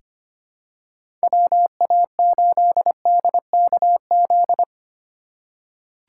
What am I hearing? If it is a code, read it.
Morse code, heard as a keyed tone, about 25 wpm, 700 Hz. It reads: WA8DKZ
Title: USA Call Signs - 6 Characters - 2x Character Spacing - 25wpm